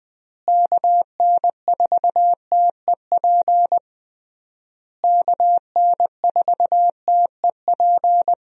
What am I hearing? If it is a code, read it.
Morse code, heard as a keyed tone, about 20 wpm, 700 Hz. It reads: KN4TEP KN4TEP
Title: USA Call Signs - 6 Characters - Repeated 3x - 20wpm